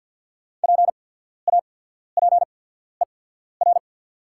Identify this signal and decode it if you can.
Morse code, heard as a keyed tone, about 48 wpm, spaced as for 14 wpm, 700 Hz. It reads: PAPER
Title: 8x Character Spacing - Top 300 Words - 50wpm